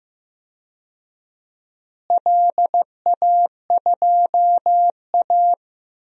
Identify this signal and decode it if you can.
Morse code, heard as a keyed tone, about 15 wpm, 700 Hz. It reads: LA2A
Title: DXCC-100 Call Signs - 4 Characters - Repeated 2x - 15wpm